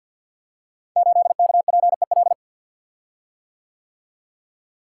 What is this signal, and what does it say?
Morse code, heard as a keyed tone, about 52 wpm, 700 Hz. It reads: 8KPEL
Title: Code Groups - 5 Letter:Numbers - 50wpm